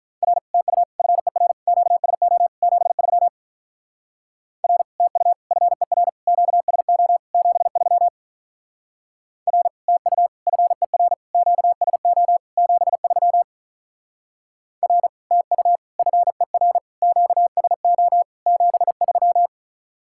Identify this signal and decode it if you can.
Morse code, heard as a keyed tone, about 43 wpm, 700 Hz. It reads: R TU FER QSO 73 R TU FER QSO 73 R TU FER QSO 73 R TU FER QSO 73
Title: Speed-Racing Common QSO Phrases Encore - 35wpm